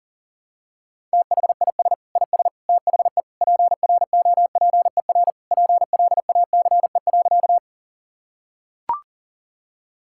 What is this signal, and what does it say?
Morse code, heard as a keyed tone, about 37 wpm, 700 Hz. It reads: THIS IS THE PROPER PLACE.
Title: Sentences from Top 1000 Words - Review Part 5 - 40wpm